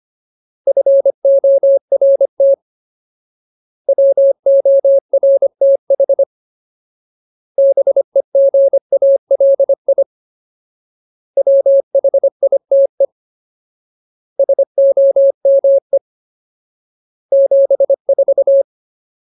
Morse code, heard as a keyed tone, about 25 wpm, 550 Hz. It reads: FORT WORTH BEGALI WHITE SOME 74
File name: Ultimate Word List / 25WPM Ultimate